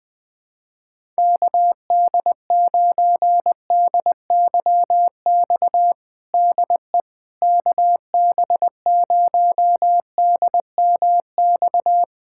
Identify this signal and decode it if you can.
Morse code, heard as a keyed tone, about 21 wpm, 700 Hz. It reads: KD9DYX DE KB0DMX
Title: USA Callsign DE Callsign 20wpm